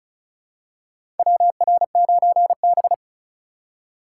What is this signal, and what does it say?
Morse code, heard as a keyed tone, about 36 wpm, 700 Hz. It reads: WR9B
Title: USA Call Signs - 4 Characters - 3x Character Spacing - 35wpm